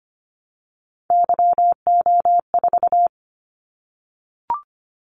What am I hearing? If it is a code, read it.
Morse code, heard as a keyed tone, about 25 wpm, 700 Hz. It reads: YO4
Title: Callsign Prefixes for Top 25 DXCC - 25wpm